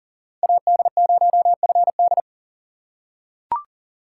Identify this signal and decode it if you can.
Morse code, heard as a keyed tone, about 40 wpm, 700 Hz. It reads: AD0FD
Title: USA Call Signs - 5 Characters - 6x Character Spacing - 40wpm